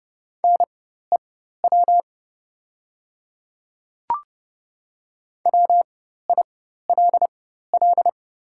Morse code, heard as a keyed tone, about 30 wpm, spaced as for 14 wpm, 700 Hz. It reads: NEW WILL
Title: 4x Character Spacing - Top 200 Words - 30wpm